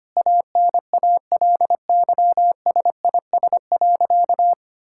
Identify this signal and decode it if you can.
Morse code, heard as a keyed tone, about 25 wpm, 700 Hz. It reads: ANALYSIS.
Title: Sentences from Top 1000 Words - Review Part 5 - 25wpm